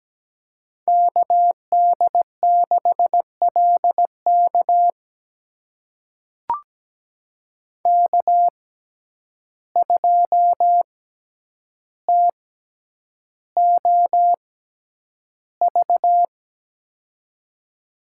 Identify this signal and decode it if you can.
Morse code, heard as a keyed tone, about 17 wpm, spaced as for 9 wpm, 700 Hz. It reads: KD6LK K2TOV
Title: USA Call Signs - 5 Characters - 6x Character Spacing - 17wpm